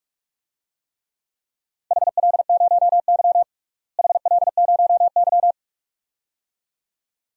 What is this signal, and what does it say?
Morse code, heard as a keyed tone, about 41 wpm, 700 Hz. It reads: HL0Y HL0Y
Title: DXCC-100 Call Signs - 4 Characters - Repeated 2x - 45wpm